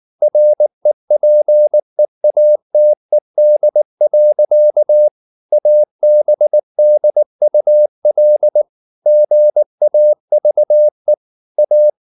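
Morse code, heard as a keyed tone, about 19 wpm, 600 Hz. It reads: REPEATED. ABDUL GAVE A
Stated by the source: Morse code at 19 wpm from CW med Gnister, reading Ken Follett's novel Never in English.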